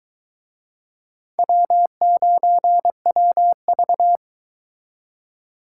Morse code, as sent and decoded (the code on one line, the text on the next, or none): .-- ----. .-- ...-
W9WV